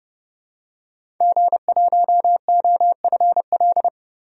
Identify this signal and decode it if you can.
Morse code, heard as a keyed tone, about 30 wpm, 690 Hz. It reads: G1OFL
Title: DXCC-100 Call Signs - 5 Characters - 30wpm